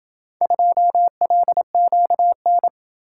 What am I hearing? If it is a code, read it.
Morse code, heard as a keyed tone, about 27 wpm, 700 Hz. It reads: 2LQN